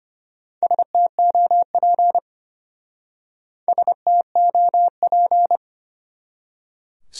Morse code, heard as a keyed tone, about 30 wpm, 700 Hz. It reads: STOP STOP
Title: Speed-Racing Top 300 Words 25wpm